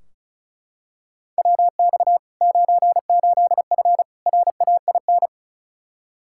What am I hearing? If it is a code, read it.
Morse code, heard as a keyed tone, about 34 wpm, 700 Hz. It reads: WX 98F RAIN
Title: Practice QSO - Exchange 2 - Info - 35wpm